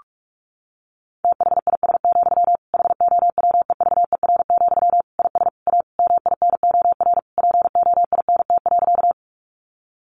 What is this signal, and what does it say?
Morse code, heard as a keyed tone, about 44 wpm, 700 Hz. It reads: THIS, HOWEVER, IS A MINOR POINT.
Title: Sentences from Top 1500 Words 45wpm